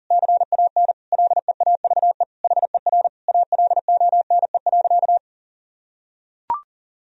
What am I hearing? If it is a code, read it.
Morse code, heard as a keyed tone, about 40 wpm, 700 Hz. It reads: CAN LEAVE HER ALONE.
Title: Sentences from Top 1000 Words - Review Part 3 - 40wpm